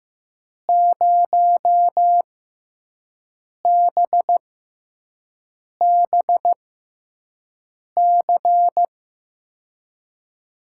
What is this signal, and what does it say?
Morse code, heard as a keyed tone, about 15 wpm, spaced as for 5 wpm, 700 Hz. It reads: 0BBC
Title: USA Call Signs - 5 Characters - 6x Character Spacing - 15wpm